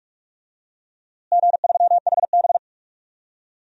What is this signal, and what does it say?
Morse code, heard as a keyed tone, about 41 wpm, 700 Hz. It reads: G3HB
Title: DXCC-100 Call Signs - 4 Characters - Repeated 2x - 45wpm